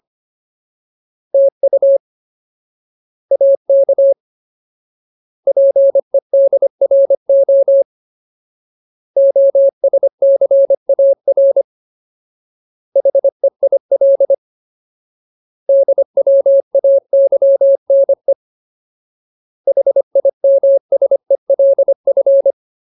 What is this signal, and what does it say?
Morse code, heard as a keyed tone, about 25 wpm, 550 Hz. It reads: TU AK PEDRO OSCAR HEIL DWAYNE HIMSELF
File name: Ultimate Word List / 25WPM Ultimate